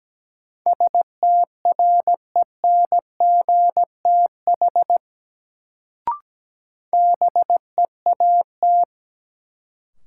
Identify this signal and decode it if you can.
Morse code, heard as a keyed tone, about 17 wpm, 700 Hz. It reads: STRENGTH BEAT
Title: Sentences from Top 1300 Words - New Word Drill Alt - 17wpm